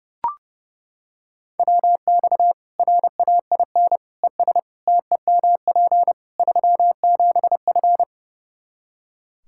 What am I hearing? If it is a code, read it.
Morse code, heard as a keyed tone, about 30 wpm, 700 Hz. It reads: WX RAIN ES TEMP 37F
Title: Practice QSO - Exchange 2 - Info - 30wpm